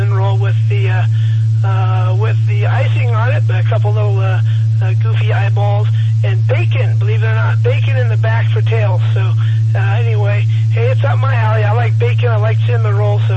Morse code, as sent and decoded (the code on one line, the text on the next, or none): none